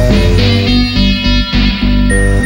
There is no code.